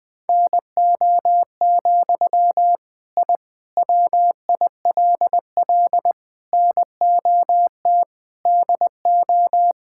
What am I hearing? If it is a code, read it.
Morse code, heard as a keyed tone, about 20 wpm, 700 Hz. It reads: NO, I WILL NOT DO
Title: Sentences from Top 1000 Words - Review Part 6 - 20wpm